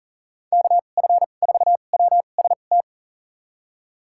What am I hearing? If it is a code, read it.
Morse code, heard as a keyed tone, about 40 wpm, spaced as for 29 wpm, 700 Hz. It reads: KF4WST
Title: USA Call Signs - 6 Characters - 2x Character Spacing - 40wpm